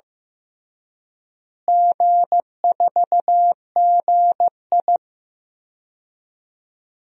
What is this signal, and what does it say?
Morse code, heard as a keyed tone, about 15 wpm, 700 Hz. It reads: G4GI
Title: DXCC-100 Call Signs - 4 Characters - Repeated 3x - 15wpm